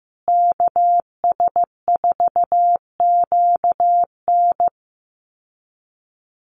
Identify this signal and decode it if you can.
Morse code, heard as a keyed tone, about 15 wpm, 700 Hz. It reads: KS4QN
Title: USA Call Signs Repeated 3x 15wpm